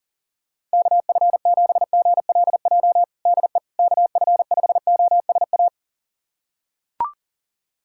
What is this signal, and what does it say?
Morse code, heard as a keyed tone, about 40 wpm, 700 Hz. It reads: KF7GLJ DE KF5OSA
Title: USA Callsign DE Callsign - 6 Characters - 40wpm